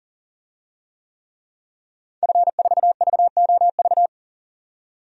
Morse code, heard as a keyed tone, about 40 wpm, 700 Hz. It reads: F4VOV
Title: DXCC-100 Call Signs - 5 Characters - 40wpm